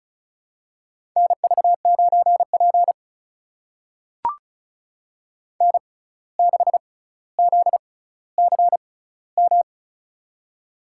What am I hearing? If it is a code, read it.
Morse code, heard as a keyed tone, about 36 wpm, spaced as for 17 wpm, 700 Hz. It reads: NV9P N6ZCM
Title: USA Call Signs - 6x Character Spacing - 35wpm